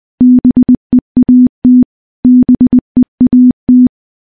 Morse code, heard as a keyed tone, about 20 wpm, 250 Hz. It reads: BEAT BEAT
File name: morse-training-20WPM